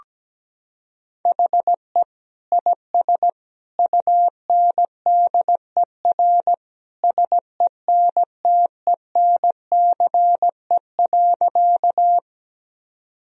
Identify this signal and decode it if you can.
Morse code, heard as a keyed tone, about 17 wpm, 700 Hz. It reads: HE IS UNDER SENTENCE.
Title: Sentences from Top 1000 Words - Review Part 1 - 17wpm